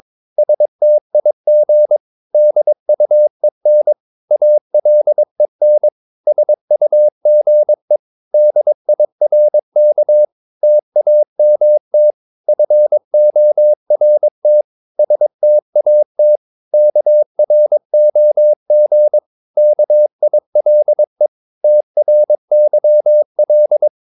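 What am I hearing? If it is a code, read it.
Morse code, heard as a keyed tone, about 22 wpm, 600 Hz. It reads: STIG DUEN ALEN SUGE DIRK TAMT FORT STAT KROG KILE TRYL
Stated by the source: Morse practice of short Danish words at 22 wpm, from CW med Gnister.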